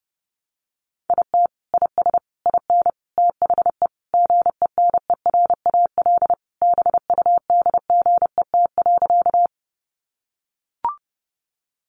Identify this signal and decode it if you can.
Morse code, heard as a keyed tone, about 30 wpm, 700 Hz. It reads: IT IS IN THE GENERAL BUDGET.
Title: Sentences from Top 1200 Words 30wpm